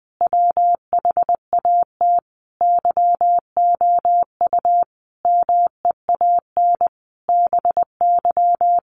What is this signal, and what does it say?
Morse code, heard as a keyed tone, about 20 wpm, 700 Hz. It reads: WHAT YOU MEAN BY
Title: Sentences from Top 1000 Words 20wpm